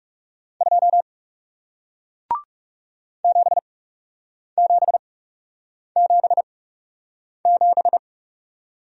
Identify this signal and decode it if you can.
Morse code, heard as a keyed tone, about 38 wpm, spaced as for 8 wpm, 700 Hz. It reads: 2 7777
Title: Speed-Racing Single Letter-Number 30wpm